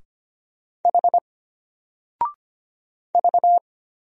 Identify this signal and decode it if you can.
Morse code, heard as a keyed tone, about 25 wpm, 700 Hz. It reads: H V
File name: Single-Letter-Number-25wpm